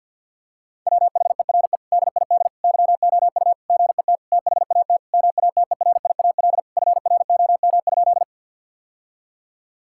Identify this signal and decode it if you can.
Morse code, heard as a keyed tone, about 50 wpm, 700 Hz. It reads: WHERE DID YOU GET THAT MATERIAL FROM?
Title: Sentences from Top 1000 Words - Review Part 5 - 50wpm